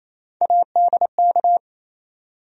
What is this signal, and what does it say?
Morse code, heard as a keyed tone, about 28 wpm, 700 Hz. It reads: ADK
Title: USA Call Sign Postfixes - 3 Letters 28wpm